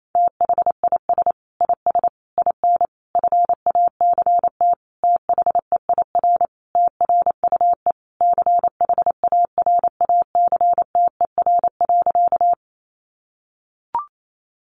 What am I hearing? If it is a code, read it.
Morse code, heard as a keyed tone, about 28 wpm, 700 Hz. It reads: THIS IS IN FACT THEIR TRUE CHARACTER.
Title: Sentences from Top 1200 Words 28wpm